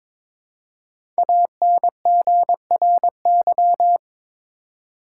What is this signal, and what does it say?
Morse code, heard as a keyed tone, about 22 wpm, 700 Hz. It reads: ANGRY
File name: 5 Letter Words - First 3 Letters Given - 22wpm